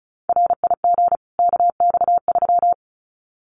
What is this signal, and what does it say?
Morse code, heard as a keyed tone, about 34 wpm, 700 Hz. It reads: RIG KX3
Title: Common QSO Phrases 35wpm